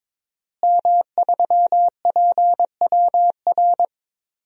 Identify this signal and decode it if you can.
Morse code, heard as a keyed tone, about 22 wpm, 700 Hz. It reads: M3PWR
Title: DXCC-100 Call Signs - 5 Characters - Repeated 2x - 22wpm